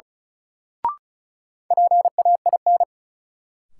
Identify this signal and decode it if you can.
Morse code, heard as a keyed tone, about 34 wpm, 700 Hz. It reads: PAIN
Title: Sentences from Top 1600 Words - New Word Drill - 35wpm